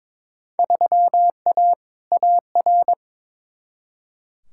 Morse code, heard as a keyed tone, about 22 wpm, 700 Hz. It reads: 3A AR
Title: Field Day Exchanges - 22wpm